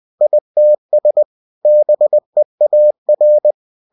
Morse code, heard as a keyed tone, about 20 wpm, 600 Hz. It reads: ITS BEAR
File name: cw-books / winnie-the-pooh / winnie-20wpm-page-0029